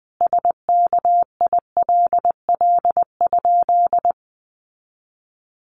Morse code, heard as a keyed tone, about 20 wpm, 700 Hz. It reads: SKILL?
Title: Sentences from Top 1200 Words 20wpm